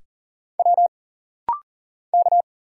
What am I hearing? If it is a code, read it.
Morse code, heard as a keyed tone, about 40 wpm, 700 Hz. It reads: W K
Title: Single Letters - Rapid-Fire 40wpm